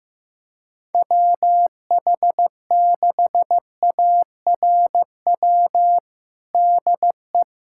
Morse code, heard as a keyed tone, about 15 wpm, 700 Hz. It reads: WH6ARW DE
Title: USA Callsign DE Callsign - 6 Characters - 15wpm